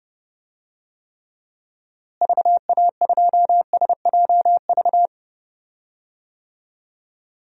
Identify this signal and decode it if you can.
Morse code, heard as a keyed tone, about 32 wpm, 700 Hz. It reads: VA2SJV
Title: DXCC-100 Call Signs - 6 Characters - Repeated 2x - 30wpm